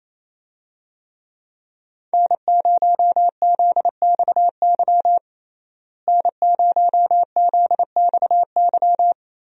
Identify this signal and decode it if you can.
Morse code, heard as a keyed tone, about 28 wpm, 700 Hz. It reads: N0ZXY N0ZXY
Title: USA Call Signs - 5 Characters - Repeated 3x - 28wpm